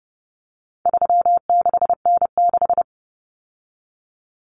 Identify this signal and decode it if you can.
Morse code, heard as a keyed tone, about 30 wpm, 700 Hz. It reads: 36N6
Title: Common Cut Numbers - 4-Digit 30wpm